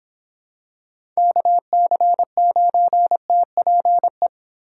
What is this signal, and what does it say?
Morse code, heard as a keyed tone, about 25 wpm, 700 Hz. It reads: KC9TPE